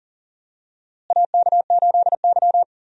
Morse code, heard as a keyed tone, about 40 wpm, 700 Hz. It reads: AK8Y
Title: USA Call Signs - 4 Characters - Repeated 2x - 40wpm